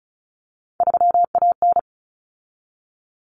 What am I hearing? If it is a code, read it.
Morse code, heard as a keyed tone, about 34 wpm, 700 Hz. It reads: 3AN
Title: Common Cut Numbers - 3-Digit 35wpm